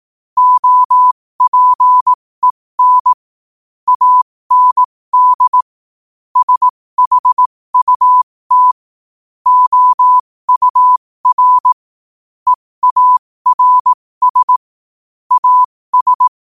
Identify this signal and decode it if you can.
Morse code, heard as a keyed tone, about 18 wpm, spaced as for 15 wpm, 1000 Hz. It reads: OPEN AND SHUT OUR EARS AS
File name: qotdfiles / 15wpm